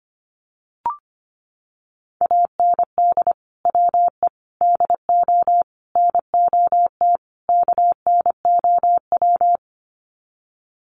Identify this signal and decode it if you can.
Morse code, heard as a keyed tone, about 25 wpm, 700 Hz. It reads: AND WE DO NOT KNOW
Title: Sets of 5 Words Encore 25wpm